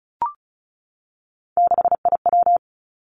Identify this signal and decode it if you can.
Morse code, heard as a keyed tone, about 36 wpm, 700 Hz. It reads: 6IW
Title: Three Challenging Letter-Number Combinations 35wpm